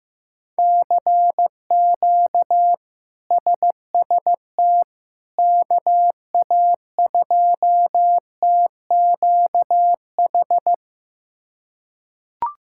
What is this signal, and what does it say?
Morse code, heard as a keyed tone, about 15 wpm, 700 Hz. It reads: CQ SST KA2TQH
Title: K1USN SST QSO Practice - 15wpm